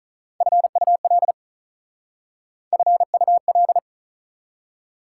Morse code, heard as a keyed tone, about 39 wpm, 690 Hz. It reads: FUL FUL
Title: Speed-Racing Prefixes and Suffixes 35wpm